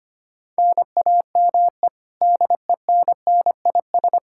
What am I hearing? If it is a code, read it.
Morse code, heard as a keyed tone, about 25 wpm, 700 Hz. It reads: NAME DENNIS
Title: Practice QSO - Exchange 1 - Beginning - 25wpm